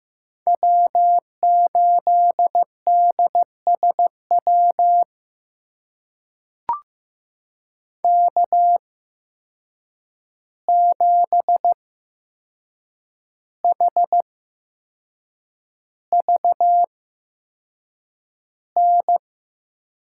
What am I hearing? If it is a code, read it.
Morse code, heard as a keyed tone, about 15 wpm, spaced as for 6 wpm, 700 Hz. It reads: W8DSW K7HVN
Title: USA Call Signs - 5 Characters - 8x Character Spacing - 15wpm